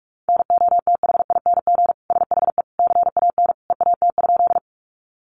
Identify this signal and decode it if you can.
Morse code, heard as a keyed tone, about 44 wpm, 700 Hz. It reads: NOTHING SHE CAN EAT?
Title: Sentences from Top 500 Words 45wpm